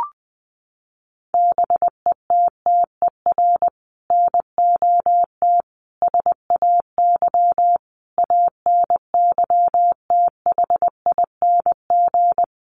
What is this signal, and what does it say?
Morse code, heard as a keyed tone, about 20 wpm, 700 Hz. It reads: BETTER NOT SAY ANYTHING
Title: Sentences from Top 1000 Words 20wpm